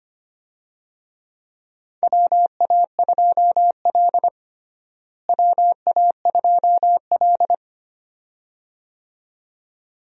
Morse code, heard as a keyed tone, about 25 wpm, 700 Hz. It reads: WA2L WA2L
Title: USA Call Signs - 4 Characters - Repeated 2x - 25wpm